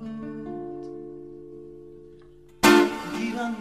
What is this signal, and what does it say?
Dial tone (North America), 350 + 440 Hz.